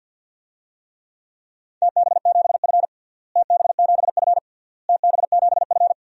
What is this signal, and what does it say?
Morse code, heard as a keyed tone, about 49 wpm, 700 Hz. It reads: TB7F TB7F TB7F
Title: DXCC-100 Call Signs - 4 Characters - Repeated 3x - 50wpm